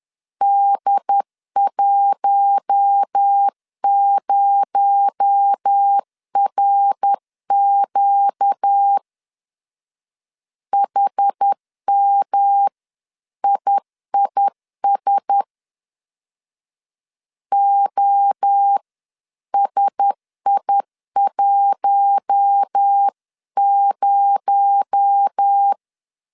Morse code, heard as a keyed tone, about 10 wpm, 800 Hz. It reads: D10RQ HMIIS OSI10
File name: CW / DET / 1 6 1500C S B 10wpm 4137